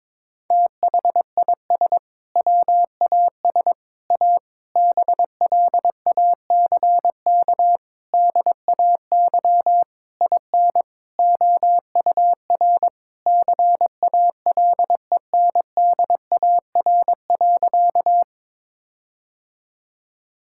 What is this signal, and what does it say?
Morse code, heard as a keyed tone, about 22 wpm, 700 Hz. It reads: THIS WAS A BLACK DAY IN OUR CALENDAR.
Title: Sentences from Top 1600 Words 22wpm